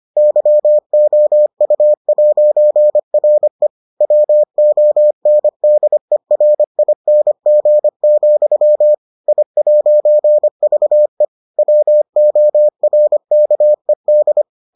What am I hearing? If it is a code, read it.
Morse code, heard as a keyed tone, about 25 wpm, 600 Hz. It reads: YOU'RE WONDERING, I'VE WORKED